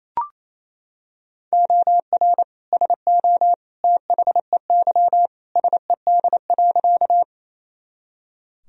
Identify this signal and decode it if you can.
Morse code, heard as a keyed tone, about 28 wpm, 700 Hz. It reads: OR SO THEY SED.
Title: Sentences from Top 100 Words and Abbreviations 28wpm